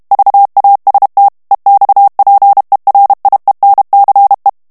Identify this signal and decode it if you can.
Morse code, heard as a keyed tone, about 32 wpm, 800 Hz. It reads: VAST EXPERIENCE
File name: ANGLAIS texte clair 2 vitesse 30 mots minute